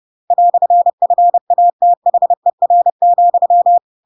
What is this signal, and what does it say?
Morse code, heard as a keyed tone, about 30 wpm, 700 Hz. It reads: "FATHER,
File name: Pinocchio / Pinocchio ch7 30 30 ik1hge